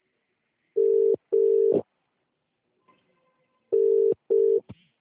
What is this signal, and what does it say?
Ringback tone (UK), 400 + 450 Hz.